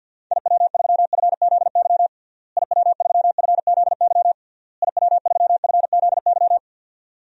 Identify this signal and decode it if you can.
Morse code, heard as a keyed tone, about 50 wpm, 700 Hz. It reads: IW3FZY IW3FZY IW3FZY
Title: DXCC-100 Call Signs - 6 Characters - Repeated 3x - 50wpm